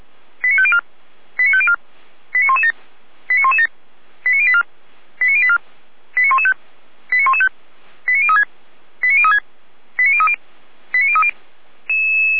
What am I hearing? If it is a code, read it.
Five-tone selective call (ZVEI), heard as quick runs of five tones, tones of about 70 ms. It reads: 89504 89504 89117 89117 89084 89084 89115 89115 89036 89036 89030 89030
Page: SDR/ZVEI